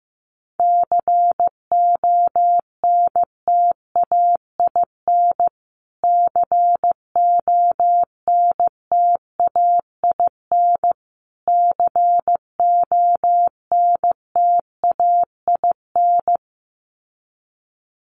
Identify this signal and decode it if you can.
Morse code, heard as a keyed tone, about 15 wpm, 700 Hz. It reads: CONTAIN CONTAIN CONTAIN
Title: Top 500 Words Repeated 15wpm